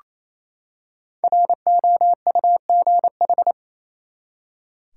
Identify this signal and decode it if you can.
Morse code, heard as a keyed tone, about 28 wpm, 700 Hz. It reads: ROUGH
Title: Five Letter Words - 28wpm